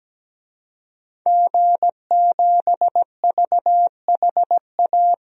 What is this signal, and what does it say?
Morse code, heard as a keyed tone, about 17 wpm, 700 Hz. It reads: G7VHA
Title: DXCC-100 Call Signs - 5 Characters - 17wpm